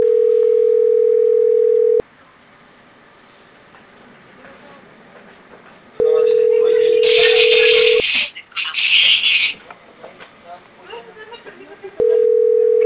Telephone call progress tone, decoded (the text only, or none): ringback tone